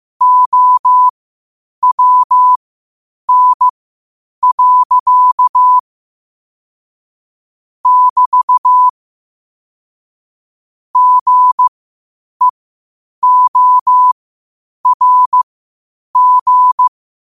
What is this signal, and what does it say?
Morse code, heard as a keyed tone, about 15 wpm, spaced as for 8 wpm, 1000 Hz. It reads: OWN. <BT> GEORG
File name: qotdfiles / 8wpm